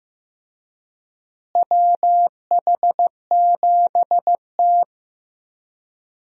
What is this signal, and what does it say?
Morse code, heard as a keyed tone, about 15 wpm, 700 Hz. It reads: WH7T
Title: USA Call Signs - 4 Characters - Repeated 2x - 15wpm